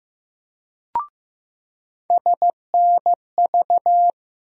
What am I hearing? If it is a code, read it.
Morse code, heard as a keyed tone, about 15 wpm, 700 Hz. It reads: SNV
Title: USA Call Sign Postfixes - 3 Letters 15wpm